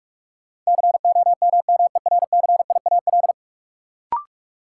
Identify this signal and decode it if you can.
Morse code, heard as a keyed tone, about 46 wpm, 700 Hz. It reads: COMMERCIAL
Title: Sentences from Top 1000 Words - Review Part 5 - New Word Drill - 45wpm